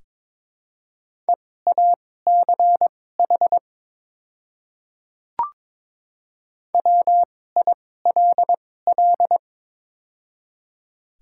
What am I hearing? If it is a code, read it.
Morse code, heard as a keyed tone, about 22 wpm, spaced as for 16 wpm, 700 Hz. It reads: EACH WILL
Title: 2x Character Spacing - Top 300 Words - 22wpm